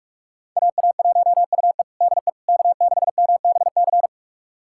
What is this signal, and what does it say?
Morse code, heard as a keyed tone, about 41 wpm, 700 Hz. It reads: AA1UE DE K6MBC
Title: USA Callsign DE Callsign - 5 Characters - 45wpm